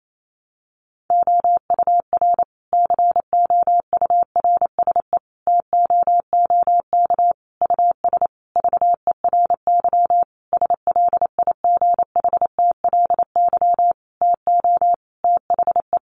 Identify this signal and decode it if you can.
Morse code, heard as a keyed tone, about 28 wpm, 700 Hz. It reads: OUR COURSE TOOK US VERY SLIGHTLY TO THE
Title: Sentences from Top 1300 Words 28wpm